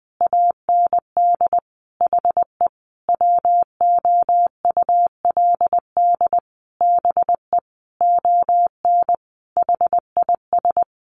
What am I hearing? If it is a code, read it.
Morse code, heard as a keyed tone, about 20 wpm, 700 Hz. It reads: AND HE WOULD BE ON HIS
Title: Sentences from Top 300 Words 20wpm